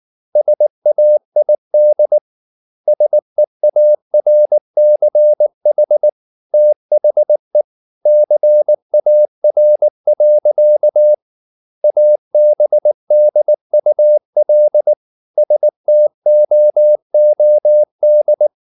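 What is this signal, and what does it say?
Morse code, heard as a keyed tone, about 19 wpm, 600 Hz. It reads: SAID SEARCH THE CAR. ABDUL STOOD
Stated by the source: Morse code at 19 wpm from CW med Gnister, reading Ken Follett's novel Never in English.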